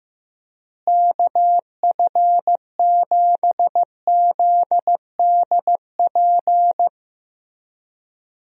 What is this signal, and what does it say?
Morse code, heard as a keyed tone, about 15 wpm, 700 Hz. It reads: KF7ZDP
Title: USA Call Signs - 6 Characters - Repeated 3x - 15wpm